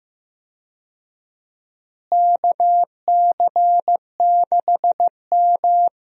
Morse code, heard as a keyed tone, about 15 wpm, 700 Hz. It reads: KC6M
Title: USA Call Signs - 4 Characters - Repeated 3x - 15wpm